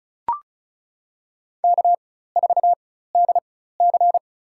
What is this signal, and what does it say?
Morse code, heard as a keyed tone, about 34 wpm, spaced as for 16 wpm, 700 Hz. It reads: K4DC
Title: USA Call Signs - 4 Characters - 4x Character Spacing - 35wpm